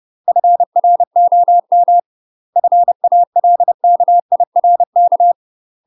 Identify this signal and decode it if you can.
Morse code, heard as a keyed tone, about 30 wpm, 700 Hz. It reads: FROM FALKIRK
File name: UK MASTER QSOs TRAINING / 30WPM / UK MASTER QSOs 0049